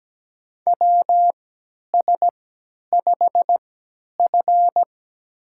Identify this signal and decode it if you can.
Morse code, heard as a keyed tone, about 17 wpm, spaced as for 10 wpm, 700 Hz. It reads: WS5F
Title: USA Call Signs - 3x Character Spacing - 17wpm